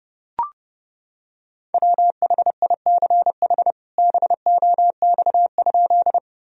Morse code, heard as a keyed tone, about 32 wpm, 700 Hz. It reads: WHICH BOX?
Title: Sentences from Top 1000 Words - Review Part 2 - 30wpm